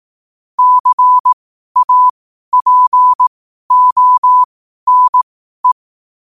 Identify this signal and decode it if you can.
Morse code, heard as a keyed tone, about 18 wpm, spaced as for 12 wpm, 1000 Hz. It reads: CAPONE
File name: qotdfiles / 12wpm